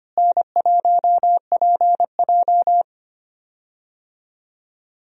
Morse code, heard as a keyed tone, about 25 wpm, 700 Hz. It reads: N1PJ
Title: USA Call Signs - 4 Characters - Repeated 2x - 25wpm